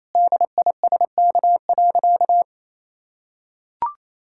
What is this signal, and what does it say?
Morse code, heard as a keyed tone, about 28 wpm, 700 Hz. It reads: DISK.
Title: Sentences from Top 1000 Words - Review Part 6 - 28wpm